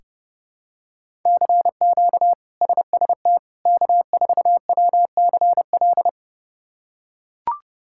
Morse code, heard as a keyed tone, about 30 wpm, 700 Hz. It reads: CQ SST K4WCL
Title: K1USN SST QSO Practice - 30wpm